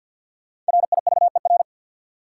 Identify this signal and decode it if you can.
Morse code, heard as a keyed tone, about 52 wpm, 700 Hz. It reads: RIVER
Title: Sentences from Top 300 Words - New Word Drill - 50wpm